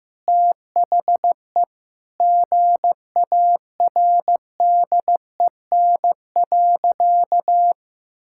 Morse code, heard as a keyed tone, about 15 wpm, 700 Hz. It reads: THE GARDEN.
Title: Sentences from Top 1000 Words - Review Part 6 - 15wpm